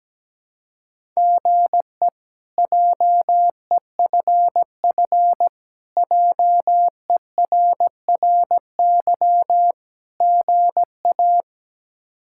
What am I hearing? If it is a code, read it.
Morse code, heard as a keyed tone, about 17 wpm, 700 Hz. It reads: GE JEFF JERRY GA
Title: K1USN SST QSO Practice - 17wpm